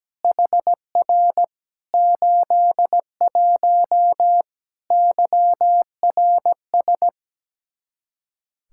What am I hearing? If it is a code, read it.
Morse code, heard as a keyed tone, about 17 wpm, 700 Hz. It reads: HR 81 YRS
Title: Practice QSO - Exchange 3 - More Info - 17wpm